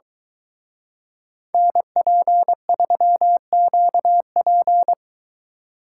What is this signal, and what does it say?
Morse code, heard as a keyed tone, about 23 wpm, 700 Hz. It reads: NP3QP